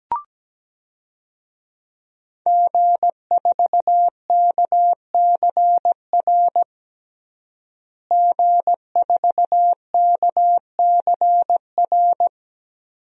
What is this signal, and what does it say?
Morse code, heard as a keyed tone, about 17 wpm, 700 Hz. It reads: G4KCR G4KCR
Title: DXCC-100 Call Signs - 5 Characters - Repeated 2x - 17wpm